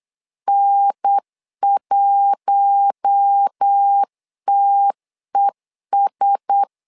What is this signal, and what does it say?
Morse code, heard as a keyed tone, about 8 wpm, 800 Hz. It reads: N1TES
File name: LEZIONI / 4 Q1S3 8wpm 3037